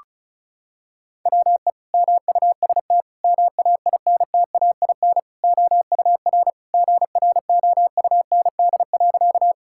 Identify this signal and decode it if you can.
Morse code, heard as a keyed tone, about 36 wpm, 700 Hz. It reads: WE MUST MAINTAIN OUR GROUND.
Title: Sentences from Top 1000 Words - Review Part 5 - 35wpm